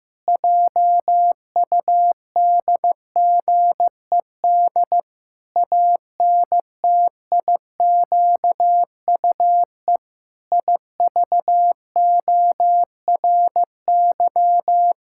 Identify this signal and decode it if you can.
Morse code, heard as a keyed tone, about 15 wpm, 700 Hz. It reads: JUDGED ANTIQUE IVORY
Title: Pangram Challenge v2 - 15wpm